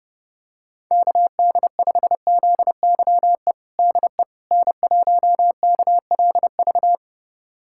Morse code, heard as a keyed tone, about 28 wpm, 700 Hz. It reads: KD5ZYE DE N1KLV
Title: USA Callsign DE Callsign 30wpm